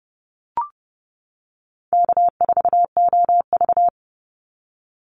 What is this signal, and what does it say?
Morse code, heard as a keyed tone, about 31 wpm, 700 Hz. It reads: K4OV
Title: Easy Call Signs - 30wpm